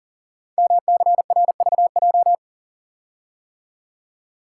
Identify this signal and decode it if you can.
Morse code, heard as a keyed tone, about 42 wpm, 700 Hz. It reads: MCRVJ
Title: Code Groups - 5 Letters - 40wpm